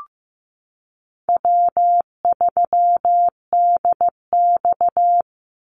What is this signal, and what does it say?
Morse code, heard as a keyed tone, about 15 wpm, 700 Hz. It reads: W3DX
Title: NA SOTA Activator Callsigns 15wpm